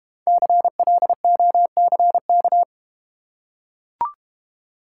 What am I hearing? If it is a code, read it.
Morse code, heard as a keyed tone, about 31 wpm, 690 Hz. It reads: CLOCK
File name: Easy Words - Speed-Racing with Hints - 25wpm